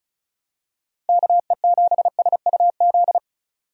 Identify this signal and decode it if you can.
Morse code, heard as a keyed tone, about 36 wpm, 700 Hz. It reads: KE7SUZ
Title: USA Call Signs - 6 Characters - Repeated 2x - 35wpm